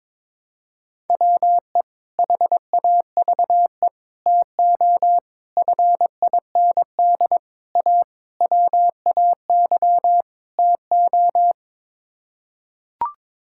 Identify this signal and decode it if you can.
Morse code, heard as a keyed tone, about 22 wpm, 700 Hz. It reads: WE HAVE TO FIND A WAY TO
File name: practice / 22wpm / Sets of 7 Words - 22wpm